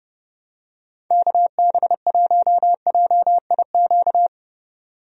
Morse code, heard as a keyed tone, about 30 wpm, 700 Hz. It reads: KB1JIQ
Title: USA Call Signs - 6 Characters - 30wpm